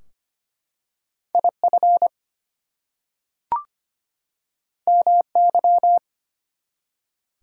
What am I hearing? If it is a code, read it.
Morse code, heard as a keyed tone, about 25 wpm, 700 Hz. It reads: IF MY
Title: Two Letter Words - 25wpm